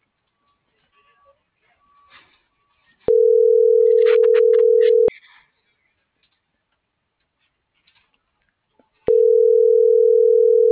Ringback tone (North America), 440 + 480 Hz.